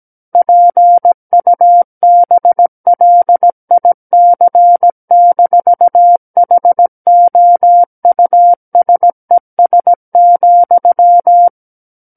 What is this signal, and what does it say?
Morse code, heard as a keyed tone, about 17 wpm, 700 Hz. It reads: PUBLIC-HOUSES,
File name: Book One / 08-Chapter 8 - 17 WPM